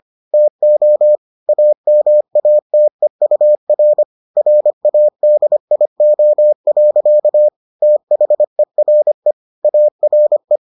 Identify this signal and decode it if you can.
Morse code, heard as a keyed tone, about 25 wpm, 600 Hz. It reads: TO AMATEUR RADIO. THERE ARE